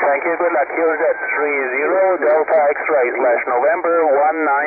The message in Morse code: none